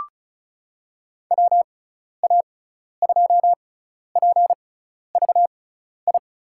Morse code, heard as a keyed tone, about 36 wpm, spaced as for 12 wpm, 690 Hz. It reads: WA2PVI
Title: USA Call Signs - 6 Characters - 6x Character Spacing - 35wpm